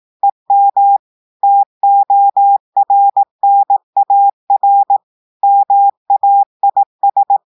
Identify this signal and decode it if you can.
Morse code, heard as a keyed tone, about 18 wpm, 800 Hz. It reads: EM TORNAR MAIS
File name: audios / portugais / portugais texte en clair 1 vitesse 18 mots minutes